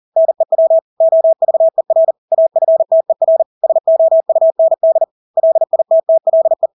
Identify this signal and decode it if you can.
Morse code, heard as a keyed tone, about 40 wpm, 650 Hz. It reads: NEW OVER AFTER SOUND LITTLE
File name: Engelske ord 40-40 wpm